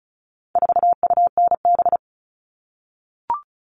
Morse code, heard as a keyed tone, about 34 wpm, 700 Hz. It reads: 4UNB
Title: All Cut Numbers - 4-Digit 35wpm